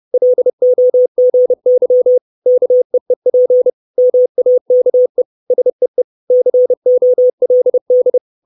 Morse code, heard as a keyed tone, about 30 wpm, 500 Hz. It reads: LOGY KEEP MAKE SEE COLD
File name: 632 Words 30WPM